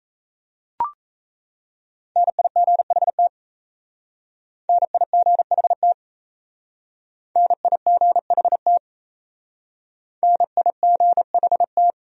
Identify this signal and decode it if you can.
Morse code, heard as a keyed tone, about 34 wpm, 700 Hz. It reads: NIGHT NIGHT NIGHT NIGHT
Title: Speed-Racing Top 300 Words 28wpm